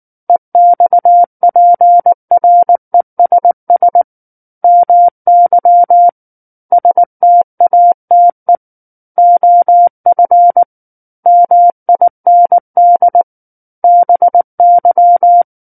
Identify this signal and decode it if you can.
Morse code, heard as a keyed tone, about 19 wpm, 700 Hz. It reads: EXPRESS MY STATE OF MIND BY
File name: Book One / 10-Chapter 10 - 19 WPM